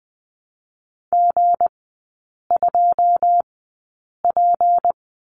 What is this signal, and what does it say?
Morse code, heard as a keyed tone, about 20 wpm, spaced as for 8 wpm, 690 Hz. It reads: G2P
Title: 205 Lesson 33e - Letter j - Three Characters 20wpm